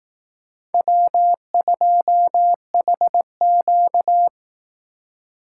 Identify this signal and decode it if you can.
Morse code, heard as a keyed tone, about 18 wpm, 700 Hz. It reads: W2HQ
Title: Speed Racing - USA 1x2 Call Signs 15wpm